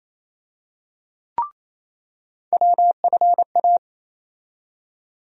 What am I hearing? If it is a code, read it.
Morse code, heard as a keyed tone, about 27 wpm, 700 Hz. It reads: WFA
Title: USA Call Sign Postfixes - 3 Letters 28wpm